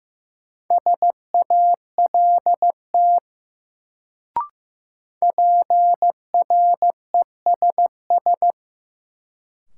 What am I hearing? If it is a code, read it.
Morse code, heard as a keyed tone, about 16 wpm, 700 Hz. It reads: SALT PRESS
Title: Sentences from Top 800 Words - New Word Drill - 15wpm